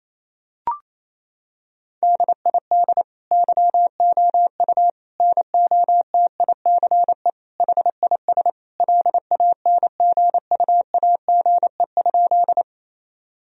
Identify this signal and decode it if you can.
Morse code, heard as a keyed tone, about 28 wpm, 700 Hz. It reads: DID YOU NOTICE HIS LANGUAGE?
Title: Sentences from Top 1000 Words - Review Part 3 - 28wpm